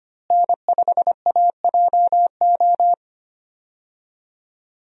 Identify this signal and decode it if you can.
Morse code, heard as a keyed tone, about 25 wpm, 700 Hz. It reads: N5AJO
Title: USA Call Signs - 5 Characters - Repeated 3x - 25wpm